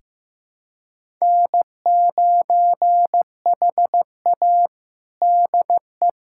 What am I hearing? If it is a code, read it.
Morse code, heard as a keyed tone, about 15 wpm, 700 Hz. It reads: N9HA DE
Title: USA Callsign DE Callsign - 4 Characters - 15wpm